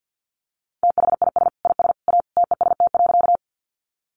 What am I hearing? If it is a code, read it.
Morse code, heard as a keyed tone, about 50 wpm, 700 Hz. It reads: THIS IS A TEST.
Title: Sentences from Top 500 Words 50wpm